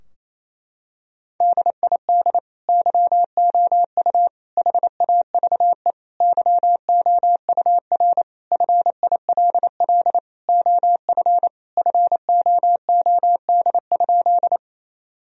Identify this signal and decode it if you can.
Morse code, heard as a keyed tone, about 28 wpm, 700 Hz. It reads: DID YOU HAVE YOUR FILL OF FOOD?
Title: Sentences from Top 1000 Words - Review Part 3 - 28wpm